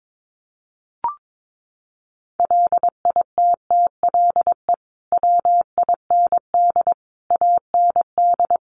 Morse code, heard as a keyed tone, about 22 wpm, 700 Hz. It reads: LITTLE WIND AND
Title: Sentences from Top 1300 Words 22wpm